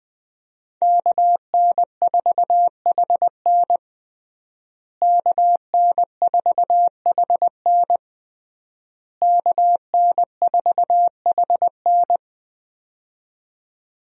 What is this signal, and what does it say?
Morse code, heard as a keyed tone, about 20 wpm, 700 Hz. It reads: KN4HN KN4HN KN4HN
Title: USA Call Signs - 5 Characters - Repeated 3x - 20wpm